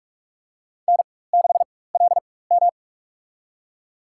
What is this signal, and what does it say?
Morse code, heard as a keyed tone, about 46 wpm, spaced as for 21 wpm, 700 Hz. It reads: N6LM
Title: USA Call Signs - 4x Character Spacing - 45wpm